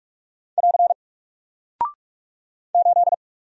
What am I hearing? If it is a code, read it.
Morse code, heard as a keyed tone, about 45 wpm, 700 Hz. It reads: <AR> 8
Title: Single Character (Letter:Number:Punctuation:Prosign) Alt 45wpm